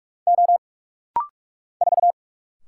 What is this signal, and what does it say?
Morse code, heard as a keyed tone, about 45 wpm, 700 Hz. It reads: O 4
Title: Single Letter-Number Alt - Rapid-Fire 45wpm